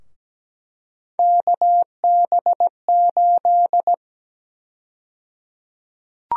Morse code, heard as a keyed tone, about 17 wpm, 700 Hz. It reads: KB8
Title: USA Call Sign Prefixes - 17wpm